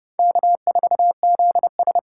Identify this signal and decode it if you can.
Morse code, heard as a keyed tone, about 36 wpm, 700 Hz. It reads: K4ZS
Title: Speed Racing - USA 1x2 Call Signs 30wpm